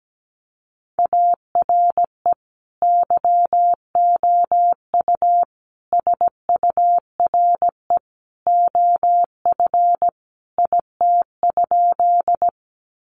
Morse code, heard as a keyed tone, about 17 wpm, 700 Hz. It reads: ARE YOU SURE OF IT?
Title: Sentences from Top 300 Words 17wpm